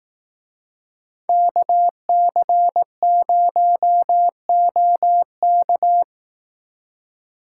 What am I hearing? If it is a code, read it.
Morse code, heard as a keyed tone, about 18 wpm, 700 Hz. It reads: KC0OK
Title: Speed Racing - USA 2x2 Call Signs 15wpm